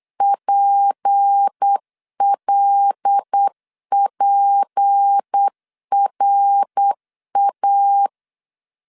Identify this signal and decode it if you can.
Morse code, heard as a keyed tone, about 8 wpm, 800 Hz. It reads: PLPRA
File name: LEZIONI / 6 PKX0 / 6 P 8wpm 3025